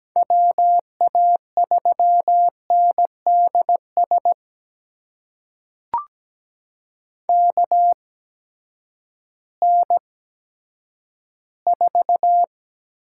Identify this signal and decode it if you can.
Morse code, heard as a keyed tone, about 17 wpm, spaced as for 10 wpm, 700 Hz. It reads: WA3NDS KN4
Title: USA Call Signs - 6 Characters - 8x Character Spacing - 17wpm